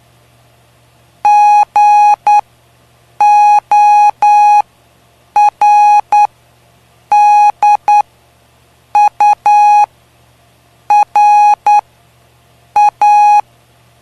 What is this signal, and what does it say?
Morse code, heard as a keyed tone, about 9 wpm, spaced as for 6 wpm, 830 Hz. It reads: GORDURA
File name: cw / py2lcd / 05-PY2LCD-Li��o 5